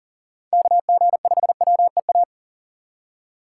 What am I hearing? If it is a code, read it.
Morse code, heard as a keyed tone, about 40 wpm, 700 Hz. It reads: KG5WEA
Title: USA Call Signs - 6 Characters - 6x Character Spacing - 40wpm